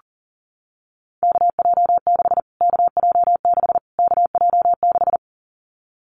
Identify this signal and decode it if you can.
Morse code, heard as a keyed tone, about 40 wpm, 700 Hz. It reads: KJ6 KJ6 KJ6
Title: Callsign Prefixes for Top 25 DXCC - Repeated 3x - 40wpm